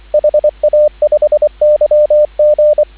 Morse code, heard as a keyed tone, about 24 wpm, 600 Hz, with radio noise. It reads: HA5YG